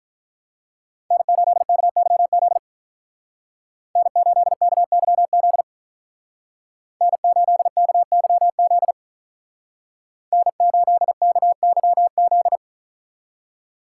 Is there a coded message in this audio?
Yes: Morse code from 1.1 to 12.6 s, heard as a keyed tone, about 47 wpm, 700 Hz. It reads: N8KYZ N8KYZ N8KYZ N8KYZ